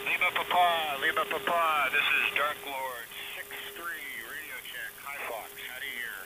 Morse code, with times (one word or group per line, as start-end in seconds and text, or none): none